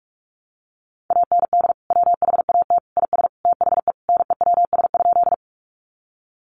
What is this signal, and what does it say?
Morse code, heard as a keyed tone, about 44 wpm, 700 Hz. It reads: AND WHAT IS THE NEWS?
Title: Sentences from Top 800 Words 45wpm